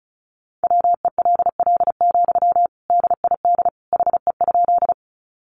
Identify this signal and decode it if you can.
Morse code, heard as a keyed tone, about 34 wpm, 700 Hz. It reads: WELL, DID HE?
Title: Sentences from Top 200 Words 35wpm